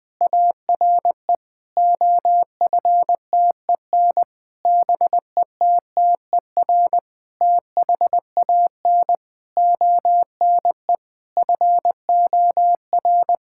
Morse code, heard as a keyed tone, about 20 wpm, 700 Hz. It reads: ARE OFTEN BETTER THAN ONE FOR
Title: Difficult Sentences with Hints - 20wpm